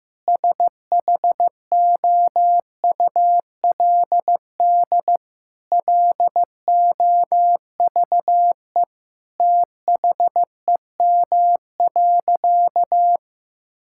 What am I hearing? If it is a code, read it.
Morse code, heard as a keyed tone, about 15 wpm, 700 Hz. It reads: SHOULD LOVE THEM.
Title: Sentences from Top 1600 Words 15wpm